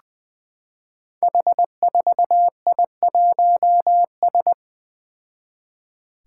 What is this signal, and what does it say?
Morse code, heard as a keyed tone, about 20 wpm, 700 Hz. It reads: H4I1S
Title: Five Challenging Letter-Number Combinations 20wpm